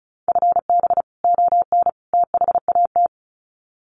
Morse code, heard as a keyed tone, about 34 wpm, 700 Hz. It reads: FB ON THAT
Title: Common QSO Phrases 35wpm